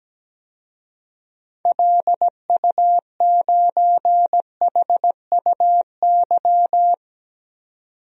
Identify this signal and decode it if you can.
Morse code, heard as a keyed tone, about 17 wpm, 700 Hz. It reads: LU9HUY
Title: DXCC-100 Call Signs - 6 Characters - 17wpm